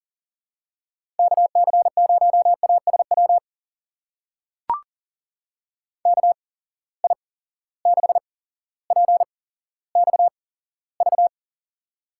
Morse code, heard as a keyed tone, about 40 wpm, spaced as for 17 wpm, 700 Hz. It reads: KC0ASW KI6PXV
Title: USA Call Signs - 6 Characters - 8x Character Spacing - 40wpm